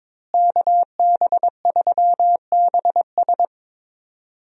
Morse code, heard as a keyed tone, about 22 wpm, 700 Hz. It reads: KB3BS
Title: USA Call Signs - 5 Characters - Repeated 3x - 22wpm